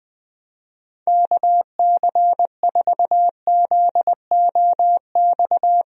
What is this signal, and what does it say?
Morse code, heard as a keyed tone, about 20 wpm, 700 Hz. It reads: KC4ZOX
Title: USA Call Signs - 6 Characters - 20wpm